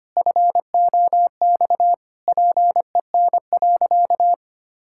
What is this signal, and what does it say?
Morse code, heard as a keyed tone, about 25 wpm, 700 Hz. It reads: FOX PEN.
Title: Pangram Challenge v2 - 25wpm